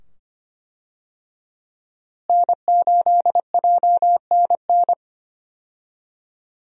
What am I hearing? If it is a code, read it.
Morse code, heard as a keyed tone, about 25 wpm, 700 Hz. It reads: N8JNN
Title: USA Call Signs - 5 Characters - Repeated 2x - 25wpm